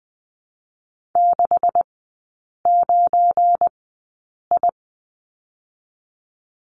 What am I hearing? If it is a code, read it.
Morse code, heard as a keyed tone, about 20 wpm, spaced as for 8 wpm, 700 Hz. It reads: 69I
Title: 218 Lesson 35e - Number 0 - Three Characters 20wpm